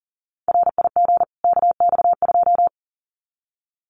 Morse code, heard as a keyed tone, about 40 wpm, 700 Hz. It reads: RIG KX2
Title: Common QSO Phrases 40wpm